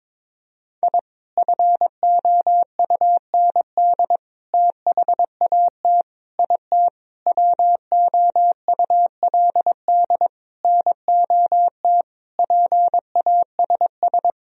Morse code, heard as a keyed tone, about 22 wpm, 700 Hz. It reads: I FOUND THAT IT WOULD NOT PASS
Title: Sentences from Top 1000 Words - Review Part 1 - 22wpm